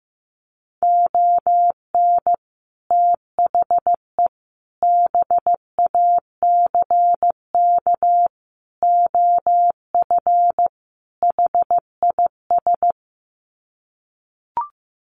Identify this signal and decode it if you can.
Morse code, heard as a keyed tone, about 15 wpm, 700 Hz. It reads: ON THE BACK OF HIS
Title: Sets of 5 Words Encore 15wpm